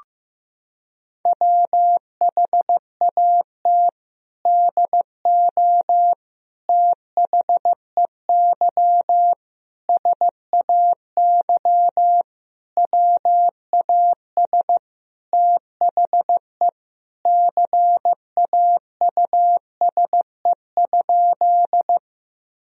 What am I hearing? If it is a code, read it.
Morse code, heard as a keyed tone, about 15 wpm, 700 Hz. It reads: WHAT DO THEY SAY WAS THE CAUSE?
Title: Sentences from Top 1000 Words - Review Part 1 - 15wpm